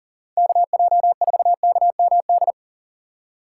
Morse code, heard as a keyed tone, about 40 wpm, 700 Hz. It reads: KJ4KMD
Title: USA Call Signs - 6 Characters - 4x Character Spacing - 40wpm